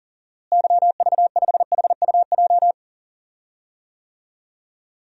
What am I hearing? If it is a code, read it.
Morse code, heard as a keyed tone, about 38 wpm, 700 Hz. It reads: YV5HUJ
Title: DXCC-100 Call Signs - 6 Characters - Repeated 2x - 40wpm